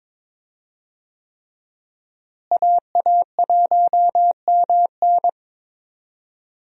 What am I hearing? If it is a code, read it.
Morse code, heard as a keyed tone, about 22 wpm, 700 Hz. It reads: AA1MN
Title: USA Call Signs - 5 Characters - Repeated 3x - 22wpm